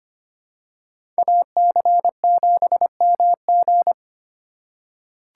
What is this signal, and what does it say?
Morse code, heard as a keyed tone, about 25 wpm, 700 Hz. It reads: AC7MG
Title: USA Call Signs - 5 Characters - 2x Character Spacing - 25wpm